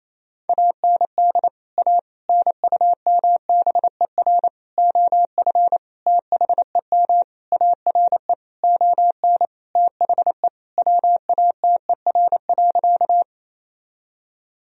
Morse code, heard as a keyed tone, about 28 wpm, 700 Hz. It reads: AND A NUMBER OF THEM ARE ON THE WATER.
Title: Sentences from Top 1000 Words - Review Part 1 - 28wpm